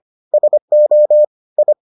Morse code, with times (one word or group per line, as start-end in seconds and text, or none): 0.3-1.2 SO
1.6-1.7 I